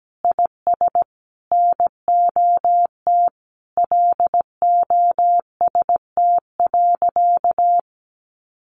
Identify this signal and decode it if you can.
Morse code, heard as a keyed tone, about 17 wpm, 700 Hz. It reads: IS NOT LOST.
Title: Sentences from Top 1400 Words 17wpm